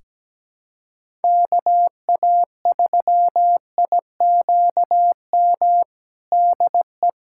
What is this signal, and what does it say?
Morse code, heard as a keyed tone, about 17 wpm, 700 Hz. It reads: KA3IQM DE
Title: USA Callsign DE Callsign - 6 Characters - 17wpm